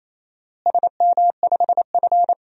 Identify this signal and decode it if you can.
Morse code, heard as a keyed tone, about 28 wpm, 700 Hz. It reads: SM5F
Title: DXCC-100 Call Signs - 4 Characters - Repeated 3x - 28wpm